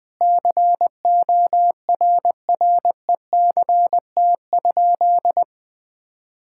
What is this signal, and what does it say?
Morse code, heard as a keyed tone, about 20 wpm, 700 Hz. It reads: CORRECT?
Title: Sentences from Top 1000 Words - Review Part 5 - 20wpm